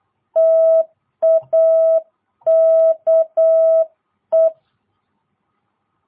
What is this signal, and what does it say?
Morse code, heard as a keyed tone, about 8 wpm, 660 Hz. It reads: TAKE